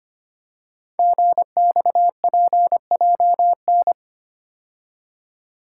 Morse code, heard as a keyed tone, about 25 wpm, 700 Hz. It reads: GXPJN
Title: Code Groups - 5 Letter:Numbers - 25wpm